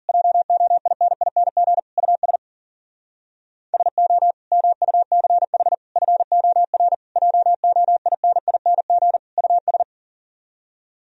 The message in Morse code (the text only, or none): JOINING US SO MUCH FOR JOINING US